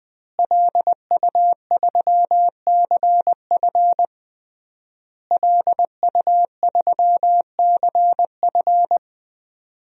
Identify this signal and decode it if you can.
Morse code, heard as a keyed tone, about 19 wpm, 700 Hz. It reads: LU3CF LU3CF
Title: DXCC-100 Call Signs - 5 Characters - Repeated 3x - 20wpm